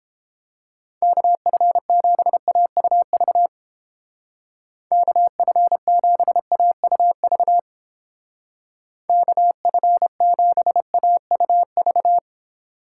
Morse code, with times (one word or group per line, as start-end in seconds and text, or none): 1.0-3.5 KF7AUV
4.9-7.6 KF7AUV
9.1-12.2 KF7AUV